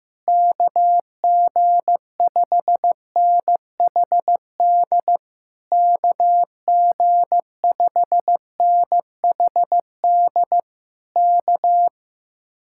Morse code, heard as a keyed tone, about 15 wpm, 700 Hz. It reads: KG5NHD KG5NHD K
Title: Speed-Racing Common QSO Phrases Encore - 15wpm